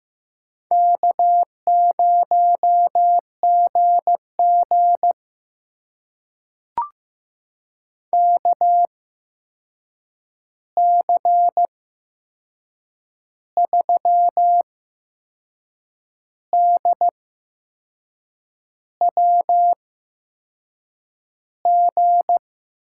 Morse code, heard as a keyed tone, about 15 wpm, spaced as for 6 wpm, 700 Hz. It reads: K0GG KC3DWG
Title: USA Call Signs - 8x Character Spacing - 15wpm